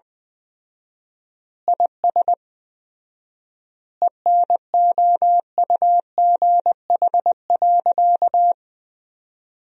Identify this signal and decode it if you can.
Morse code, heard as a keyed tone, about 19 wpm, 700 Hz. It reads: IS ENOUGH.
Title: 4x Word-Spacing - Sentences from Top 400 Words - 20wpm